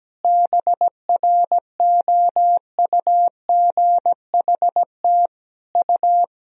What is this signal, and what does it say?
Morse code, heard as a keyed tone, about 17 wpm, 700 Hz. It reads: BROUGHT U
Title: Sentences from Top 500 Words and Abbreviations 17wpm